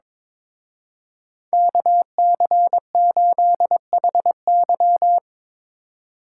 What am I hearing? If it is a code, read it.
Morse code, heard as a keyed tone, about 24 wpm, 700 Hz. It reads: KC8HY
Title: Speed Racing - USA 2x2 Call Signs 22wpm